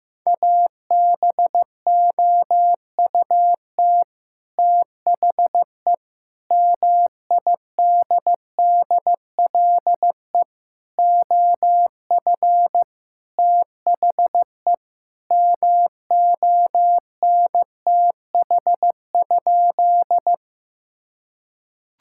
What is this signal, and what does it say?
Morse code, heard as a keyed tone, about 16 wpm, 700 Hz. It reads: ABOUT THE MIDDLE OF THE MONTH?
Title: Sentences from Top 1000 Words - Review Part 6 - 15wpm